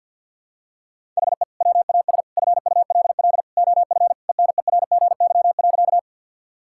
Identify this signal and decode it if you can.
Morse code, heard as a keyed tone, about 52 wpm, 700 Hz. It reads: HE WAS FULL OF ENERGY.
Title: Sentences from Top 1000 Words - Review Part 6 - 50wpm